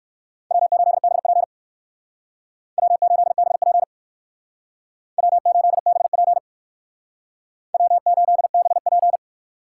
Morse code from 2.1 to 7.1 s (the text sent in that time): W8BP W8BP